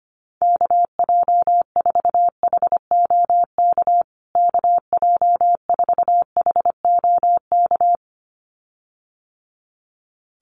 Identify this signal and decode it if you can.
Morse code, heard as a keyed tone, about 25 wpm, 700 Hz. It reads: KJ4HOK KJ4HOK
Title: USA Call Signs Repeated 2x 25wpm